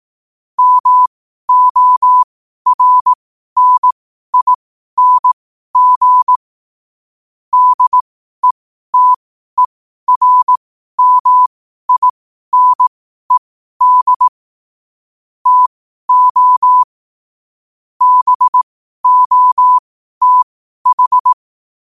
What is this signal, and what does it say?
Morse code, heard as a keyed tone, about 18 wpm, spaced as for 12 wpm, 1000 Hz. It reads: MORNING DETERMINED TO BOTH
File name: qotdfiles / 12wpm